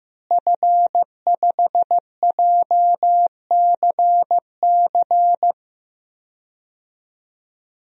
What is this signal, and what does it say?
Morse code, heard as a keyed tone, about 15 wpm, 700 Hz. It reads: F5JCC
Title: DXCC-100 Call Signs - 5 Characters - Repeated 3x - 15wpm